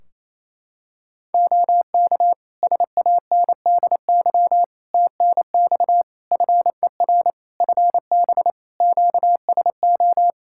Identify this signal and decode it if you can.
Morse code, heard as a keyed tone, about 27 wpm, 700 Hz. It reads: OK SANDY TNX FER FB QSO
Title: K1USN SST QSO Practice 28wpm